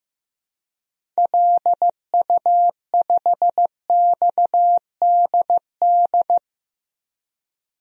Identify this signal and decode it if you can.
Morse code, heard as a keyed tone, about 15 wpm, 700 Hz. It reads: LU5XDD
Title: DXCC-100 Call Signs - 6 Characters - 15wpm